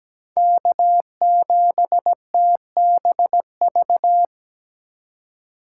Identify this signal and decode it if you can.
Morse code, heard as a keyed tone, about 17 wpm, 700 Hz. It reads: K7TBV
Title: USA Call Signs - 5 Characters - 3x Character Spacing - 17wpm